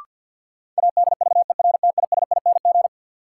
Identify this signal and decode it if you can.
Morse code, heard as a keyed tone, about 53 wpm, 700 Hz. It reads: ADVERTISING
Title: Sentences from Top 1000 Words - Review Part 6 - New Word Drill - 50wpm